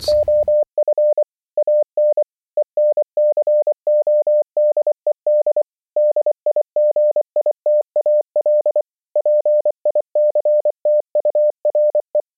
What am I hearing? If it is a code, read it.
Morse code, heard as a keyed tone, about 25 wpm, 600 Hz. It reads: OF AN ENCODED DIGITAL PICTURE
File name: audio notizie / radia s24 n297 radiohelsinki raggam dataradio pt4